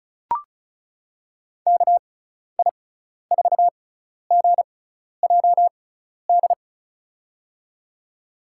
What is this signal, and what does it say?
Morse code, heard as a keyed tone, about 36 wpm, spaced as for 12 wpm, 700 Hz. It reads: KI4GJD
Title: USA Call Signs - 6 Characters - 6x Character Spacing - 35wpm